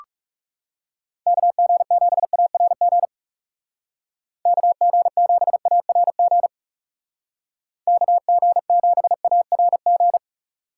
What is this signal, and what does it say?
Morse code, heard as a keyed tone, about 36 wpm, 700 Hz. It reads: KG7ARG KG7ARG KG7ARG
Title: Speed Racing - USA 2x3 Call Signs 30wpm